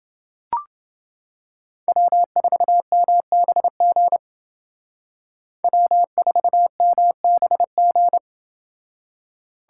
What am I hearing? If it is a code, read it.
Morse code, heard as a keyed tone, about 29 wpm, 700 Hz. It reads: W4MBG W4MBG